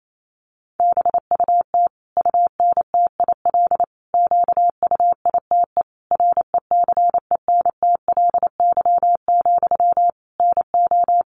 Morse code, heard as a keyed tone, about 28 wpm, 700 Hz. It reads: BUT UNTIL QUITE RECENTLY, NO
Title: Sentences from Top 1200 Words 28wpm